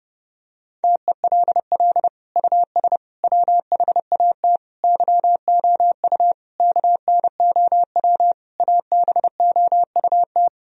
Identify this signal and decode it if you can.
Morse code, heard as a keyed tone, about 30 wpm, 700 Hz. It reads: TELL US WHAT YOU KNOW ABOUT
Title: Sentences from Top 1000 Words - Review Part 4 - 30wpm